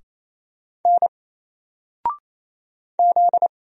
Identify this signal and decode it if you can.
Morse code, heard as a keyed tone, about 28 wpm, 700 Hz. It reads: N Z
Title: Single Letter-Number Alt 28wpm